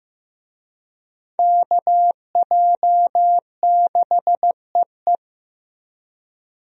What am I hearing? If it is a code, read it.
Morse code, heard as a keyed tone, about 15 wpm, 700 Hz. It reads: KJ6EE
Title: USA Call Signs - 5 Characters - 15wpm